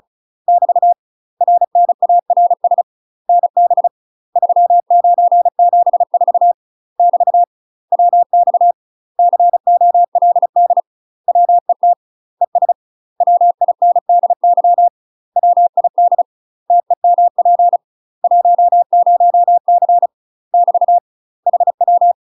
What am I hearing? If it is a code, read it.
Morse code, heard as a keyed tone, about 36 wpm, 700 Hz. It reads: <BT> RNARS NB 3974 <BT> WX COLD WET ES WINDY WID TEMP 10C <BT> HW
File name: UK MASTER QSOs TRAINING / 35WPM / UK MASTER QSOs 0017